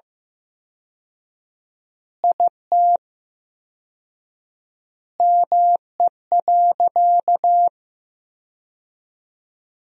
Morse code, heard as a keyed tone, about 15 wpm, 700 Hz. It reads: IT ME.
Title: 4x Word-Spacing - Sentences from Top 200 Words - 15wpm